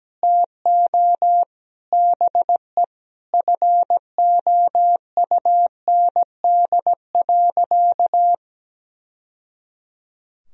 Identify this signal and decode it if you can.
Morse code, heard as a keyed tone, about 17 wpm, 700 Hz. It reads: TO BE FOUND.
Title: Sentences from Top 1000 Words - Review Part 3 - 17wpm